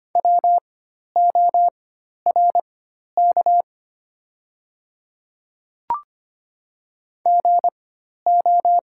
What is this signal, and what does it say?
Morse code, heard as a keyed tone, about 25 wpm, spaced as for 12 wpm, 700 Hz. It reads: WORK GO
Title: 4x Character Spacing - Top 200 Words - 25wpm